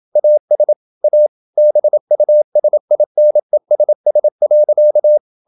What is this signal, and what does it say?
Morse code, heard as a keyed tone, about 27 wpm, 600 Hz. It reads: AS A BUSINESS.
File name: Never 0027wpm